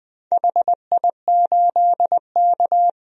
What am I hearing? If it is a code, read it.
Morse code, heard as a keyed tone, about 21 wpm, 700 Hz. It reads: HI8K
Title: DXCC-100 Call Signs - 4 Characters - 20wpm